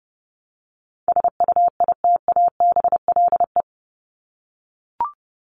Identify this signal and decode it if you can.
Morse code, heard as a keyed tone, about 29 wpm, 700 Hz. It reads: SUITABLE
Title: Top 1251-1500 Words 30wpm